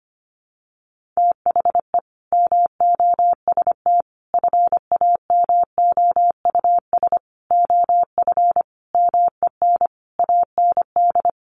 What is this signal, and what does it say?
Morse code, heard as a keyed tone, about 25 wpm, 700 Hz. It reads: THE MOST FAMOUS OF MEN AND
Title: Sentences from Top 1500 Words 25wpm